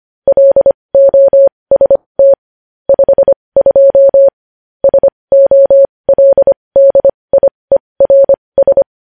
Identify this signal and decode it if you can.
Morse code, heard as a keyed tone, about 25 wpm, spaced as for 20 wpm, 550 Hz. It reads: LOST 52 SOLDIERS
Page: news-headlines-in-morse-code-at-20-wpm